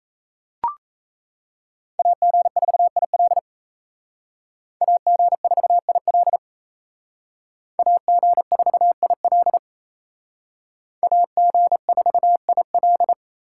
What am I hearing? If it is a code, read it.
Morse code, heard as a keyed tone, about 34 wpm, 700 Hz. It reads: AG4IL AG4IL AG4IL AG4IL
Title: Speed Racing - USA 2x2 Call Signs 28wpm